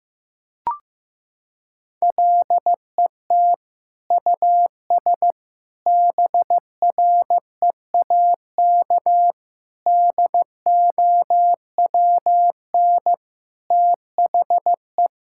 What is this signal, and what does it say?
Morse code, heard as a keyed tone, about 15 wpm, 700 Hz. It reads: LET US BREAK DOWN THE
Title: Sentences from Top 1000 Words - Review Part 6 - 15wpm